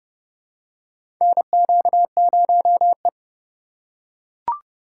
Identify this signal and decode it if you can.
Morse code, heard as a keyed tone, about 31 wpm, 700 Hz. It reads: NQ0E
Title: USA Call Signs - 4 Characters - 3x Character Spacing - 30wpm